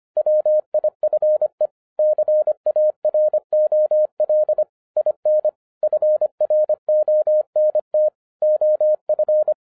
Morse code, heard as a keyed tone, about 25 wpm, 610 Hz. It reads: WIFE CAROL IN FRONT OF
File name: 2011-07-mp3 / n44 JOHN K3TN 25wpm